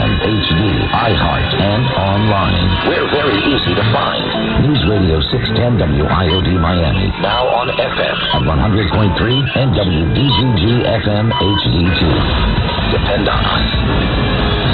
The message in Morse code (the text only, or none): none